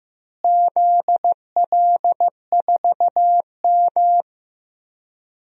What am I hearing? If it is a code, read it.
Morse code, heard as a keyed tone, about 15 wpm, 700 Hz. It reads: ZL4M
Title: DXCC-100 Call Signs - 4 Characters - Repeated 2x - 15wpm